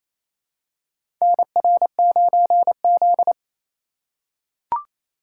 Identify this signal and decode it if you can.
Morse code, heard as a keyed tone, about 28 wpm, 700 Hz. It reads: NR9Z
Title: USA Call Signs - 4 Characters - 3x Character Spacing - 28wpm